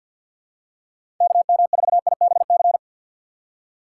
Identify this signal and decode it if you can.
Morse code, heard as a keyed tone, about 47 wpm, 700 Hz. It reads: KM4IBC
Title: USA Call Signs - 2x Character Spacing - 50wpm